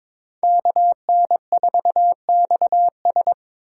Morse code, heard as a keyed tone, about 22 wpm, 700 Hz. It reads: KN4XS
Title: USA Call Signs - 5 Characters - 2x Character Spacing - 22wpm